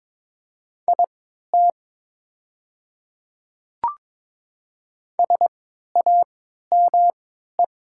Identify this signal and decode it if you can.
Morse code, heard as a keyed tone, about 22 wpm, spaced as for 12 wpm, 700 Hz. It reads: IT SAME
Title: 3x Character Spacing - Top 200 Words - 22wpm